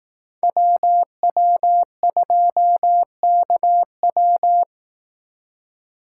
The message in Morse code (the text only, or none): WW2KW